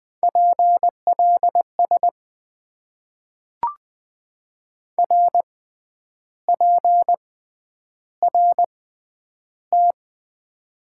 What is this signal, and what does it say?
Morse code, heard as a keyed tone, about 20 wpm, spaced as for 9 wpm, 700 Hz. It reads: PLS RPRT
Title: Top 90 QSO Elements - 6x Character Spacing - 20wpm